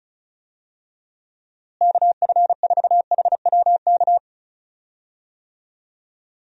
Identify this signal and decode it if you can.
Morse code, heard as a keyed tone, about 34 wpm, 700 Hz. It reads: KF4HWK
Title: USA Call Signs - 6 Characters - Repeated 2x - 35wpm